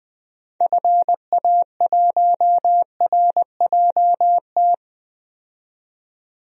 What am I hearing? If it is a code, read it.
Morse code, heard as a keyed tone, about 21 wpm, 700 Hz. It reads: FA1RJT
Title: DXCC-100 Call Signs - 6 Characters - 20wpm